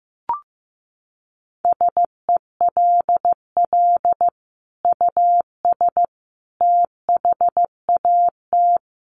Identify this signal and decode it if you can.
Morse code, heard as a keyed tone, about 15 wpm, 700 Hz. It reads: SELL US THAT
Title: Sentences from Top 1100 Words 15wpm